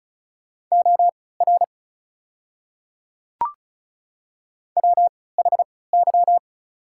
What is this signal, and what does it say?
Morse code, heard as a keyed tone, about 34 wpm, spaced as for 20 wpm, 700 Hz. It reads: OR WHY
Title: 3x Character Spacing - Top 300 Words - 35wpm